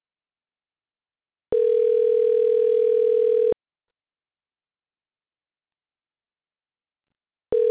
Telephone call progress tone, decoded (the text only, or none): ringback tone